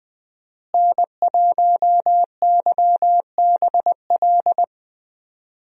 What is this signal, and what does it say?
Morse code, heard as a keyed tone, about 20 wpm, 700 Hz. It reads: N1YBL
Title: USA Call Signs - 5 Characters - Repeated 2x - 20wpm